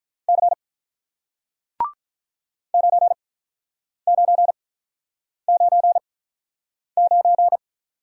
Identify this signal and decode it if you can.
Morse code, heard as a keyed tone, about 48 wpm, spaced as for 9 wpm, 700 Hz. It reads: C 9999